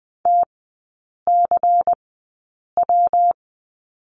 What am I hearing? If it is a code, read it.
Morse code, heard as a keyed tone, about 20 wpm, spaced as for 8 wpm, 700 Hz. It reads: TCW
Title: 114 Lesson 19e - Letter w - Three Characters 20wpm